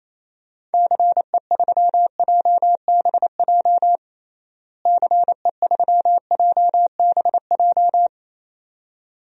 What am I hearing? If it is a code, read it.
Morse code, heard as a keyed tone, about 28 wpm, 700 Hz. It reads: CE3JBJ CE3JBJ
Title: DXCC-100 Call Signs - 6 Characters - Repeated 2x - 28wpm